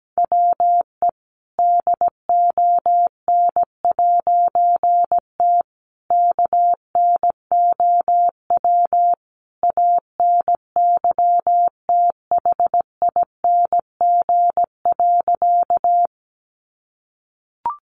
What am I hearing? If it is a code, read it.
Morse code, heard as a keyed tone, about 17 wpm, 700 Hz. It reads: WE DON'T KNOW ANYTHING.
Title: A Sound Of Thunder 17wpm